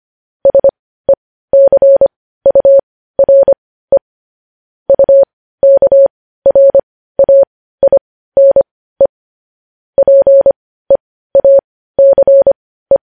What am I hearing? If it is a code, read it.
Morse code, heard as a keyed tone, about 25 wpm, spaced as for 15 wpm, 550 Hz. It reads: SECURE UKRAINE PEACE
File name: podcast / News headlines-1x-25x15